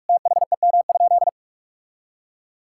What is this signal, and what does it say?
Morse code, heard as a keyed tone, about 46 wpm, 700 Hz. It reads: THEM?
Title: Sentences from Top 1000 Words - Review Part 6 - 45wpm